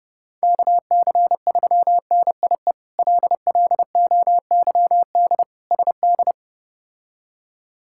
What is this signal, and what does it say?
Morse code, heard as a keyed tone, about 30 wpm, 700 Hz. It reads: KC3NIE LLOYD SD
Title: K1USN SST QSO Practice - 30wpm